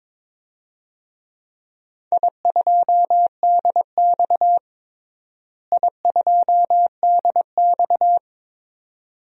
Morse code, heard as a keyed tone, about 21 wpm, 700 Hz. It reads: I2DX I2DX
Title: DXCC-100 Call Signs - 4 Characters - Repeated 2x - 22wpm